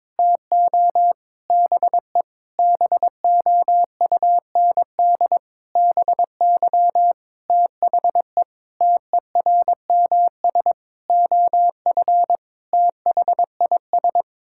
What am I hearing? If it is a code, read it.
Morse code, heard as a keyed tone, about 22 wpm, 700 Hz. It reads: TO BE BOUND BY THE TERMS OF THIS
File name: Sets of 9 Words - 22wpm